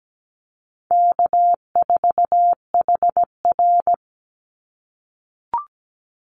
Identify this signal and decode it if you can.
Morse code, heard as a keyed tone, about 17 wpm, 700 Hz. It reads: K4HR
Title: Easy Call Signs - 17wpm